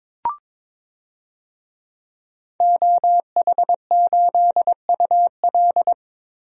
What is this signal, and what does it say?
Morse code, heard as a keyed tone, about 22 wpm, 700 Hz. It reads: OH8UL
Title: DXCC-100 Call Signs - 5 Characters - Repeated 2x - 22wpm